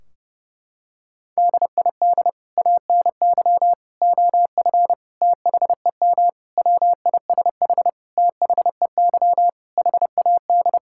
Morse code, heard as a keyed tone, about 30 wpm, 700 Hz. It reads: DID ANY OF THEM WISH THEY HAD
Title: Sentences from Top 1600 Words 30wpm